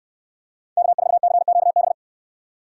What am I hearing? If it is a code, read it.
Morse code, heard as a keyed tone, about 67 wpm, 700 Hz. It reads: K4CQB